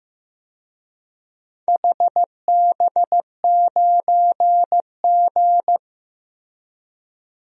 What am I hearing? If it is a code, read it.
Morse code, heard as a keyed tone, about 16 wpm, 700 Hz. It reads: HB9G
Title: DXCC-100 Call Signs - 4 Characters - Repeated 2x - 15wpm